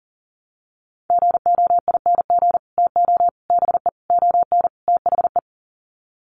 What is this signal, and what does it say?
Morse code, heard as a keyed tone, about 40 wpm, 700 Hz. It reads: GOING TO BE ON THE
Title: Sets of 5 Words Encore2 40wpm